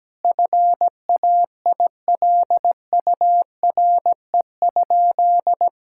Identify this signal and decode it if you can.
Morse code, heard as a keyed tone, about 17 wpm, 700 Hz. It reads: FAILURE?
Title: Sentences from Top 1000 Words - Review Part 6 - 17wpm